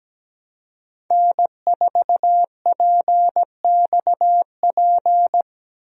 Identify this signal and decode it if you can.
Morse code, heard as a keyed tone, about 17 wpm, 700 Hz. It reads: N4PXP
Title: USA Call Signs - 5 Characters - Repeated 3x - 17wpm